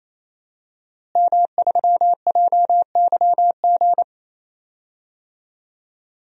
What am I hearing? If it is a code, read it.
Morse code, heard as a keyed tone, about 27 wpm, 700 Hz. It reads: M3JYG
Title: DXCC-100 Call Signs - 5 Characters - Repeated 3x - 28wpm